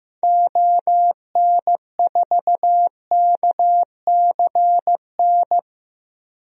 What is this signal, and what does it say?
Morse code, heard as a keyed tone, about 15 wpm, 700 Hz. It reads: ON4KCN
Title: DXCC-100 Call Signs - 6 Characters - 15wpm